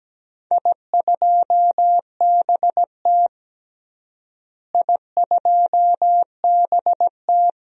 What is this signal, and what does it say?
Morse code, heard as a keyed tone, about 17 wpm, 700 Hz. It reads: I2BT I2BT
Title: DXCC-100 Call Signs - 4 Characters - Repeated 3x - 17wpm